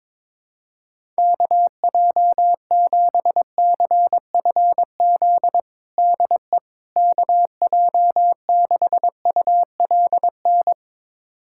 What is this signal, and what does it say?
Morse code, heard as a keyed tone, about 22 wpm, 700 Hz. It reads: KJ7CFZ DE KJ6ULN
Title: USA Callsign DE Callsign - 6 Characters - 22wpm